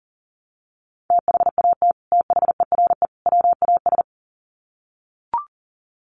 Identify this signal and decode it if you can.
Morse code, heard as a keyed tone, about 40 wpm, 700 Hz. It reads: THAT THERE WAS
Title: Sets of 3 Words 40wpm